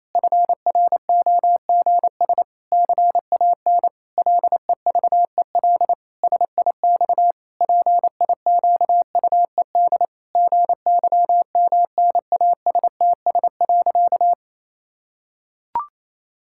Morse code, heard as a keyed tone, about 28 wpm, 700 Hz. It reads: FROGS CAN LEVEL SIX PIQUED GYMNASTS.
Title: Pangram Challenge v2 - 28wpm